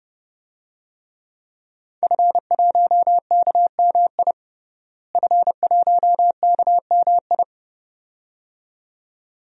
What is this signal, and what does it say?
Morse code, heard as a keyed tone, about 30 wpm, 700 Hz. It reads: F1KMI F1KMI
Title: DXCC-100 Call Signs - 5 Characters - Repeated 2x - 30wpm